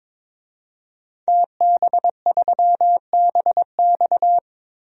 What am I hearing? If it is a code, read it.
Morse code, heard as a keyed tone, about 22 wpm, 700 Hz. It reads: TB3BX
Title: DXCC-100 Call Signs - 5 Characters - 22wpm